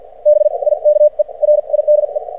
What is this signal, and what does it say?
Morse code, heard as a keyed tone, about 53 wpm, spaced as for 40 wpm, 600 Hz. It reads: BIKE RIDE
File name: archived / CRSnet